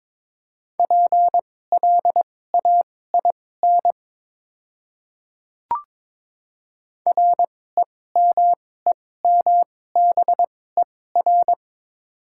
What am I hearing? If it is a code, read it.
Morse code, heard as a keyed tone, about 23 wpm, spaced as for 16 wpm, 700 Hz. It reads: PLAIN REMEMBER
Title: 2x Character Spacing - Top 500 Words - 22wpm